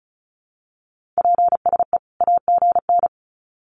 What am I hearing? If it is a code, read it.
Morse code, heard as a keyed tone, about 34 wpm, 700 Hz. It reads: PSE AGN
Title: Common QSO Phrases 35wpm